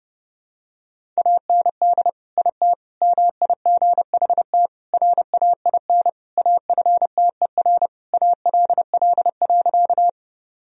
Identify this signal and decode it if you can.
Morse code, heard as a keyed tone, about 30 wpm, 700 Hz. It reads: AND IT MIGHT RAIN AFTER ALL.
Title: Sentences from Top 1000 Words - Review Part 1 - 30wpm